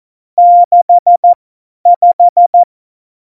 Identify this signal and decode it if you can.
Morse code, heard as a keyed tone, about 14 wpm, spaced as for 10 wpm, 700 Hz. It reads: 65
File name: Morsekurs / Witze / Witz 08. Dezember 2020, 50 F70